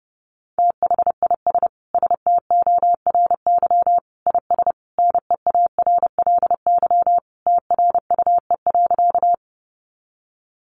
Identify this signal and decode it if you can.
Morse code, heard as a keyed tone, about 28 wpm, 700 Hz. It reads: THIS STORY IS NEARLY TRUE.
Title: Sentences from Top 1300 Words 30wpm